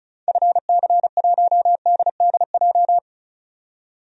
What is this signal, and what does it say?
Morse code, heard as a keyed tone, about 34 wpm, 700 Hz. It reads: FC1DDJ
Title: DXCC-100 Call Signs - 6 Characters - 35wpm